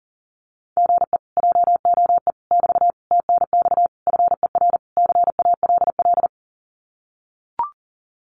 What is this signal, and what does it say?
Morse code, heard as a keyed tone, about 39 wpm, 700 Hz. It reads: GE JOE <BT> TNX FER CALL
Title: Common QSO Phrases Encore 40wpm